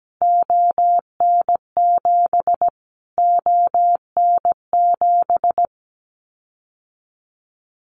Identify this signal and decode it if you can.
Morse code, heard as a keyed tone, about 17 wpm, 700 Hz. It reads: ON7 ON7
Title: Callsign Prefixes for Top 25 DXCC - Repeated 3x - 17wpm